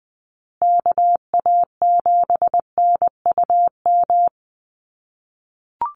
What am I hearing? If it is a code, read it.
Morse code, heard as a keyed tone, about 20 wpm, 700 Hz. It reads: KA7NUM
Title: USA Call Signs Repeated 2x 20wpm